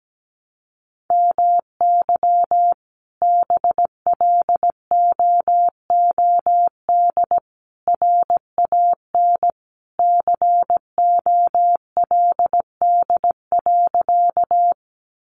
Morse code, heard as a keyed tone, about 17 wpm, 700 Hz. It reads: MY BLOOD RAN COLD.
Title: Sentences from Top 1100 Words 17wpm